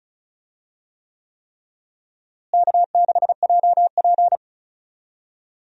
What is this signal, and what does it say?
Morse code, heard as a keyed tone, about 34 wpm, 700 Hz. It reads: K6JP
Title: USA Call Signs - 4 Characters - Repeated 3x - 35wpm